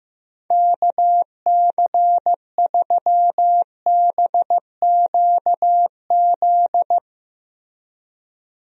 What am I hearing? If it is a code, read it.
Morse code, heard as a keyed tone, about 15 wpm, 700 Hz. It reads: KC3BQZ
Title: USA Call Signs - 6 Characters - Repeated 2x - 15wpm